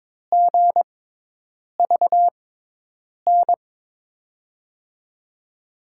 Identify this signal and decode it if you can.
Morse code, heard as a keyed tone, about 22 wpm, spaced as for 8 wpm, 700 Hz. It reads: GVN
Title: USA Call Signs - 5 Characters - 6x Character Spacing - 22wpm